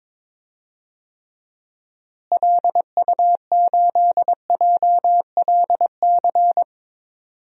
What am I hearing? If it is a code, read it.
Morse code, heard as a keyed tone, about 22 wpm, 700 Hz. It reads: LU8JLC
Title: DXCC-100 Call Signs - 6 Characters - Repeated 2x - 22wpm